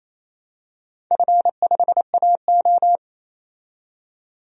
Morse code, heard as a keyed tone, about 28 wpm, 690 Hz. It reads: F5AO
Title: DXCC-100 Call Signs - 4 Characters - 28wpm